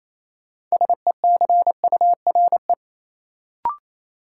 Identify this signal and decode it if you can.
Morse code, heard as a keyed tone, about 28 wpm, 700 Hz. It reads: SECURE
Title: Sentences from Top 1200 Words - New Word Drill Alt - 28wpm